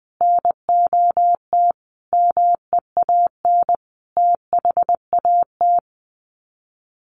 Morse code, heard as a keyed tone, about 20 wpm, 700 Hz. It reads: NOT MEAN THAT
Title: Sets of 5 Words Encore2 20wpm